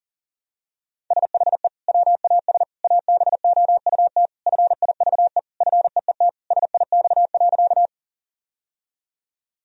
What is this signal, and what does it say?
Morse code, heard as a keyed tone, about 40 wpm, 700 Hz. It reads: SHE WAS ABOUT FIVE FEET SIX.
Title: Sentences from Top 1000 Words - Review Part 2 - 40wpm